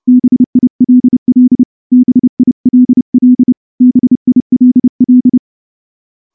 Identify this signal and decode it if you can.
Morse code, heard as a keyed tone, about 28 wpm, 250 Hz. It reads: BILL BILL BILL
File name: morse-training-30WPM